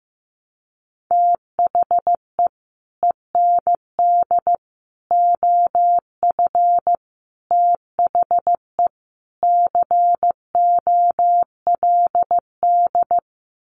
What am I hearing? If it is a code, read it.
Morse code, heard as a keyed tone, about 15 wpm, 700 Hz. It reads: THE END OF THE COLD
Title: Sets of 5 Words 15wpm